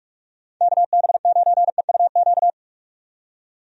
Morse code, heard as a keyed tone, about 46 wpm, 700 Hz. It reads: KB9EUQ
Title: USA Call Signs - 6 Characters - 45wpm